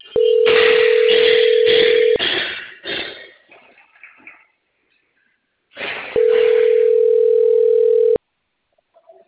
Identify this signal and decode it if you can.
Ringback tone (North America), 440 + 480 Hz.